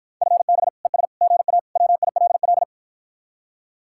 Morse code, heard as a keyed tone, about 52 wpm, 700 Hz. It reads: FB ES GA WILL